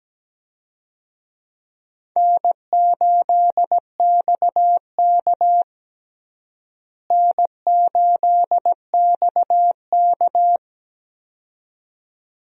Morse code, heard as a keyed tone, about 17 wpm, 700 Hz. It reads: N8XK N8XK
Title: USA Call Signs - 4 Characters - Repeated 2x - 17wpm